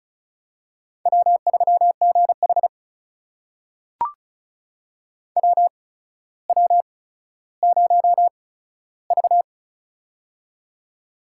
Morse code, heard as a keyed tone, about 36 wpm, spaced as for 15 wpm, 700 Hz. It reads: W3GH WW0V
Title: USA Call Signs - 4 Characters - 8x Character Spacing - 35wpm